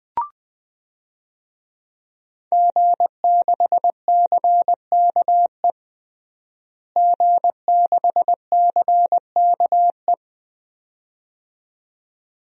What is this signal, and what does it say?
Morse code, heard as a keyed tone, about 20 wpm, 700 Hz. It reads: G6CKE G6CKE
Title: DXCC-100 Call Signs - 5 Characters - Repeated 2x - 20wpm